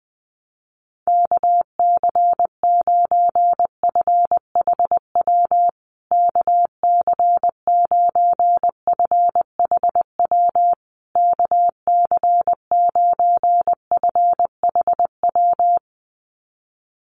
Morse code, heard as a keyed tone, about 20 wpm, 700 Hz. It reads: KC9FHW KC9FHW KC9FHW
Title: USA Call Signs Repeated 3x 20wpm